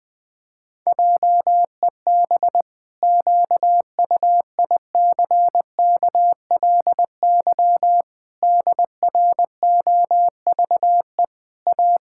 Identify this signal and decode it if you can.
Morse code, heard as a keyed tone, about 21 wpm, 700 Hz. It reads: JEB QUICKLY DROVE A
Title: Pangram Challenge v2 - 20wpm